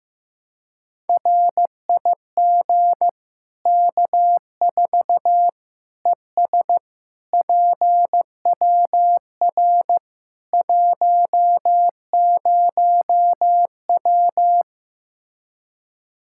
Morse code, heard as a keyed tone, about 15 wpm, 700 Hz. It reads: RIG K4 ES PWR 10W
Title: Practice QSO - Exchange 2 - Info - 15wpm